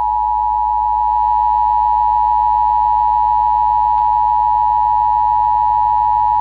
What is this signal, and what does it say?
Emergency Alert System (SAME) tones (the 853 and 960 Hz attention tone). It is heard from 0.5 to 5.5 s.